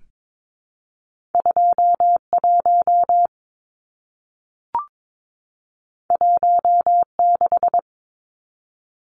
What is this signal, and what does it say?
Morse code, heard as a keyed tone, about 22 wpm, 700 Hz. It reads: 21 16
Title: Numbers - 2-Digit 22wpm